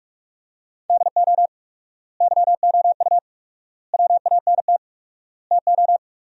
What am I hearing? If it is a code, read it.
Morse code, heard as a keyed tone, about 45 wpm, 700 Hz. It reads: DO YOU WANT TO
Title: 4x Word-Spacing - Sentences from Top 300 Words - 45wpm